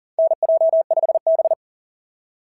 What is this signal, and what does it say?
Morse code, heard as a keyed tone, about 40 wpm, 680 Hz. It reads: NJ5B
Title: SOTA QSO Practice 40wpm